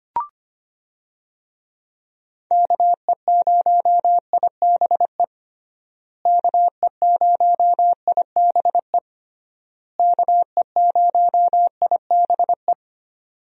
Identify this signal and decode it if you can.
Morse code, heard as a keyed tone, about 25 wpm, 700 Hz. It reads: KE0IBE KE0IBE KE0IBE
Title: USA Call Signs - 6 Characters - Repeated 3x - 25wpm